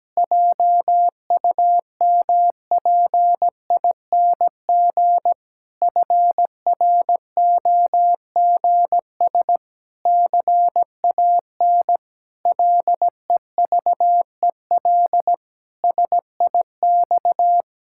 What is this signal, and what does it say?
Morse code, heard as a keyed tone, about 17 wpm, 700 Hz. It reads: JUMPING FROGS CAN LEVEL SIX
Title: Pangram Challenge v2 - 17wpm